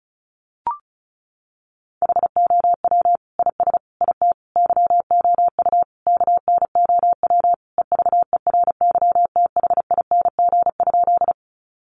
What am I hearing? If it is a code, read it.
Morse code, heard as a keyed tone, about 34 wpm, 700 Hz. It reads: HOW IS IT YOU KNOW EVERYTHING?
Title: Sentences from Top 700 Words 35wpm